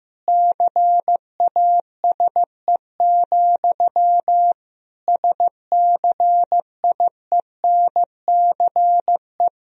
Morse code, heard as a keyed tone, about 15 wpm, 700 Hz. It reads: CASE, SCIENCE
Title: Sentences from Top 1000 Words - Review Part 2 - 15wpm